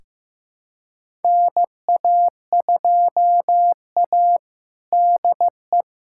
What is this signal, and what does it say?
Morse code, heard as a keyed tone, about 16 wpm, 700 Hz. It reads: NA2A DE
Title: USA Callsign DE Callsign - 4 Characters - 15wpm